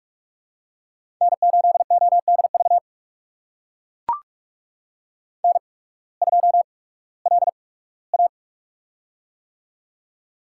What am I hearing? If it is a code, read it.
Morse code, heard as a keyed tone, about 44 wpm, spaced as for 21 wpm, 700 Hz. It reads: N8ODV N2LA
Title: USA Call Signs - 8x Character Spacing - 45wpm